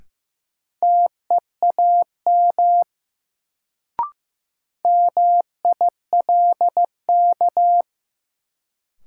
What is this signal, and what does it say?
Morse code, heard as a keyed tone, about 15 wpm, 700 Hz. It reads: TEAM MILK
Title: Sentences from Top 1500 Words - New Word Drill Alt - 15wpm